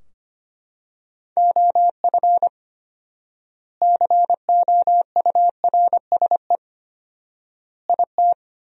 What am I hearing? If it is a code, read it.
Morse code, heard as a keyed tone, about 25 wpm, 700 Hz. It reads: OF COURSE IT
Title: 4x Word-Spacing - Sentences from Top 500 Words - 25wpm